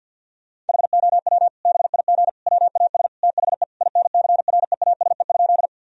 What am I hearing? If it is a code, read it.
Morse code, heard as a keyed tone, about 52 wpm, 700 Hz. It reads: HOW BIG WAS THE INCREASE?
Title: Sentences from Top 1000 Words - Review Part 4 - 50wpm